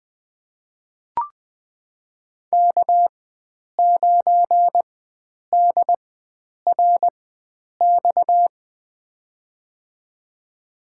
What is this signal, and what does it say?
Morse code, heard as a keyed tone, about 21 wpm, spaced as for 9 wpm, 700 Hz. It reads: K9DRX
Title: USA Call Signs - 5 Characters - 4x Character Spacing - 20wpm